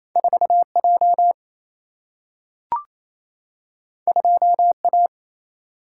Morse code, heard as a keyed tone, about 28 wpm, 700 Hz. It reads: 4 J 2 A
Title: Single-Letter Single-Number 28wpm